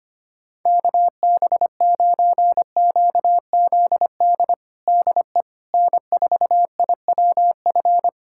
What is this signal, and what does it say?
Morse code, heard as a keyed tone, about 25 wpm, 700 Hz. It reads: KB9QZD DE N4IWF
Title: USA Callsign DE Callsign 25wpm